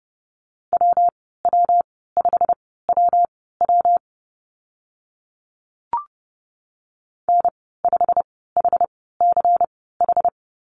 Morse code, heard as a keyed tone, about 30 wpm, spaced as for 17 wpm, 700 Hz. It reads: WW5WW N5HCH
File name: Lesson 19g - Letter w - Call Signs 30wpm